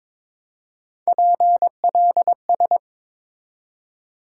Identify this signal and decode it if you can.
Morse code, heard as a keyed tone, about 22 wpm, 700 Hz. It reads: PLS
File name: Top 90 QSO Elements - 3x Character Spacing - 22wpm